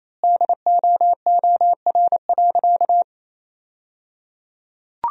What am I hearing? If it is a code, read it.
Morse code, heard as a keyed tone, about 28 wpm, 700 Hz. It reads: DOOR.
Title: 3x Word-Spacing - Sentences from Top 400 Words - 28wpm